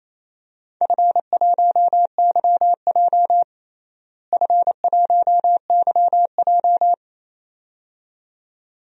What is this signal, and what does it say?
Morse code, heard as a keyed tone, about 28 wpm, 700 Hz. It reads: F1YJ F1YJ
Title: DXCC-100 Call Signs - 4 Characters - Repeated 2x - 28wpm